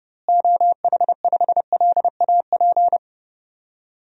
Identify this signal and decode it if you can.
Morse code, heard as a keyed tone, about 32 wpm, 700 Hz. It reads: OH5LAP
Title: DXCC-100 Call Signs - 6 Characters - Repeated 3x - 30wpm